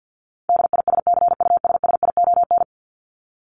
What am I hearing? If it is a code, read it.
Morse code, heard as a keyed tone, about 50 wpm, 700 Hz. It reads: DISCUSSION
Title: Top 1001-1250 Words 50wpm